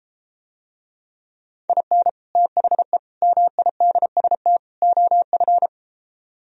Morse code, heard as a keyed tone, about 32 wpm, 700 Hz. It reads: IN THE MIDST OF